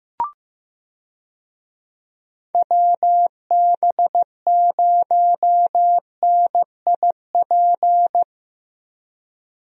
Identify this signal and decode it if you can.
Morse code, heard as a keyed tone, about 15 wpm, 700 Hz. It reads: WB0NIP
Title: USA Call Signs - 6 Characters - Repeated 2x - 15wpm